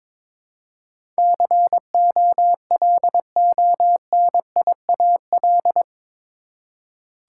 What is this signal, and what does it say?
Morse code, heard as a keyed tone, about 22 wpm, 700 Hz. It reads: COLONIAL
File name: practice / 22wpm / Eight Letter Words - 22wpm